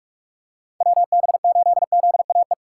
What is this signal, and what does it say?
Morse code, heard as a keyed tone, about 41 wpm, 700 Hz. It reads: WB8ZAE
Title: USA Call Signs - 6 Characters - 3x Character Spacing - 45wpm